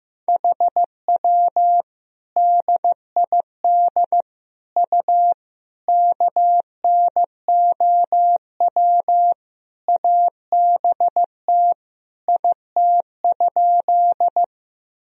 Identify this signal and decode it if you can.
Morse code, heard as a keyed tone, about 15 wpm, 700 Hz. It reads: HW DID U KNOW ABT IT?
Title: Sentences from Top 100 Words and Abbreviations 15wpm